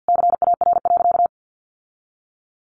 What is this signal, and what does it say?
Morse code, heard as a keyed tone, about 50 wpm, 700 Hz. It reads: CAR.
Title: Sentences from Top 1500 Words 50wpm